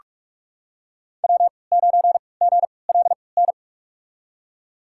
Morse code, heard as a keyed tone, about 46 wpm, spaced as for 26 wpm, 700 Hz. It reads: W9GLN
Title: USA Call Signs - 5 Characters - 3x Character Spacing - 45wpm